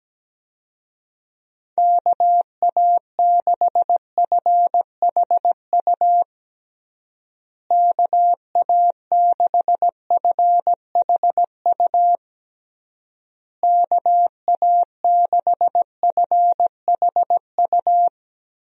Morse code, heard as a keyed tone, about 17 wpm, 700 Hz. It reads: KA6FHU KA6FHU KA6FHU
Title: USA Call Signs - 6 Characters - Repeated 3x - 17wpm